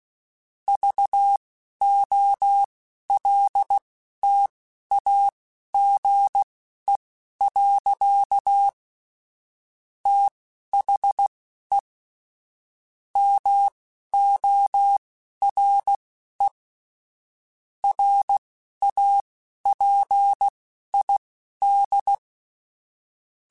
Morse code, heard as a keyed tone, about 16 wpm, spaced as for 11 wpm, 800 Hz. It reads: VOLTAGE. THE MORE RAPID
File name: Technical Subjects / Technical Subjects-mp3s / XC @10